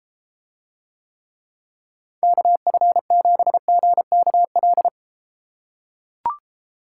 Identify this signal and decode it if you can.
Morse code, heard as a keyed tone, about 30 wpm, 700 Hz. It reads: KF7GKL